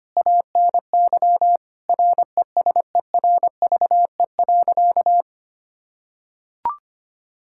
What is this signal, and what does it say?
Morse code, heard as a keyed tone, about 25 wpm, 700 Hz. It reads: ANY RESERVE.
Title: Sentences from Top 1000 Words - Review Part 6 - 25wpm